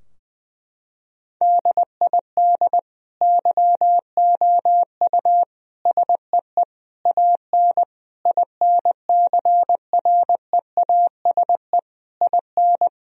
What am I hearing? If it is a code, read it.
Morse code, heard as a keyed tone, about 20 wpm, 700 Hz. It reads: DID YOU SEE AN INCREASE IN
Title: Sentences from Top 1000 Words - Review Part 4 - 20wpm